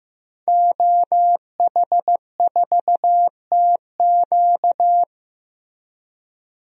Morse code, heard as a keyed tone, about 15 wpm, 700 Hz. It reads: OH4TQ
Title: DXCC-100 Call Signs - 5 Characters - 15wpm